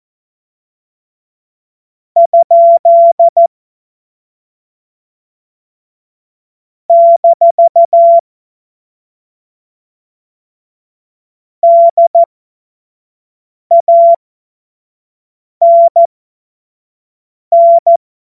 Morse code, heard as a keyed tone, about 14 wpm, spaced as for 5 wpm, 680 Hz. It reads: ? - DANN
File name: Witz-26-August-2025-25-F70